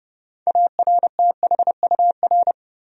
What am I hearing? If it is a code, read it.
Morse code, heard as a keyed tone, about 32 wpm, 700 Hz. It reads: ARTHUR
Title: USA Common Names - 30wpm